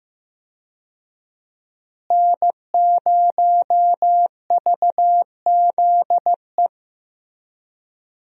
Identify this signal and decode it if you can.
Morse code, heard as a keyed tone, about 15 wpm, 700 Hz. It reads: N0VZE
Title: USA Call Signs - 5 Characters - Repeated 3x - 15wpm